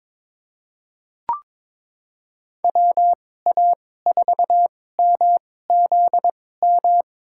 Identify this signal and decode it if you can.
Morse code, heard as a keyed tone, about 22 wpm, spaced as for 16 wpm, 700 Hz. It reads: WA4MZM
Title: USA Call Signs - 6 Characters - 2x Character Spacing - 22wpm